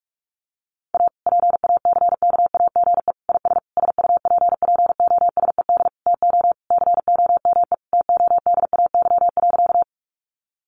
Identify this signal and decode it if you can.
Morse code, heard as a keyed tone, about 44 wpm, 700 Hz. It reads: A PACKAGE IS SUPPOSED TO COME TODAY.
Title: Easy Phrases 45wpm